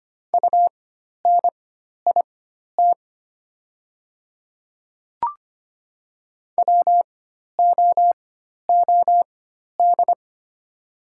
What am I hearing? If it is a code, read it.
Morse code, heard as a keyed tone, about 25 wpm, spaced as for 12 wpm, 700 Hz. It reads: UNIT WOOD
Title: 4x Character Spacing - Top 500 Words - 25wpm